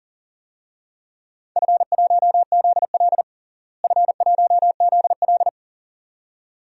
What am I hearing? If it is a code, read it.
Morse code, heard as a keyed tone, about 40 wpm, 700 Hz. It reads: F1ZL F1ZL
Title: DXCC-100 Call Signs - 4 Characters - Repeated 2x - 40wpm